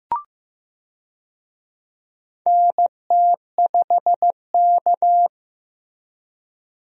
Morse code, heard as a keyed tone, about 15 wpm, 700 Hz. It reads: NT5K
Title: USA Call Signs - 4 Characters - Repeated 2x - 15wpm